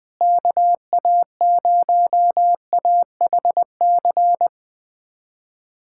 Morse code, heard as a keyed tone, about 21 wpm, 700 Hz. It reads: KA0AHC
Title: USA Callsign DE Callsign 20wpm